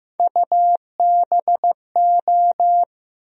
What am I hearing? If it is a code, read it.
Morse code, heard as a keyed tone, about 15 wpm, 700 Hz. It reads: UBO
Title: USA Call Sign Postfixes - 3 Letters 15wpm